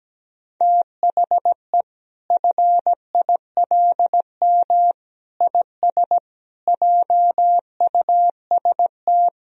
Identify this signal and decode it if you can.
Morse code, heard as a keyed tone, about 17 wpm, 700 Hz. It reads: THE FILM IS JUST
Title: Sentences from Top 1000 Words - Review Part 3 - 17wpm